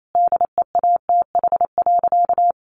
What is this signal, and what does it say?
Morse code, heard as a keyed tone, about 28 wpm, 700 Hz. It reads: DEATH.
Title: Sentences from Top 1500 Words 28wpm